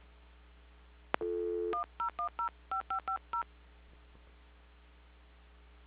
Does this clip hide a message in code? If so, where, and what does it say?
Yes: DTMF from 1.7 to 3.4 s, heard as telephone touch tones, tones of about 110 ms. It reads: 10105550